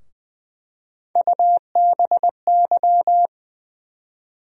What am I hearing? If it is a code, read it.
Morse code, heard as a keyed tone, about 20 wpm, 700 Hz. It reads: UBY
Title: USA Call Sign Postfixes - 3 Letters 20wpm